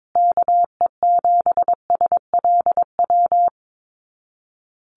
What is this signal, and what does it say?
Morse code, heard as a keyed tone, about 22 wpm, 700 Hz. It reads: KE7SLW
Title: USA Call Signs Repeated 2x 22wpm